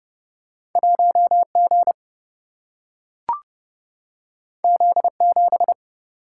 Morse code, heard as a keyed tone, about 30 wpm, 700 Hz. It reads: 1 G Z 7
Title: Single-Letter Single-Number 30wpm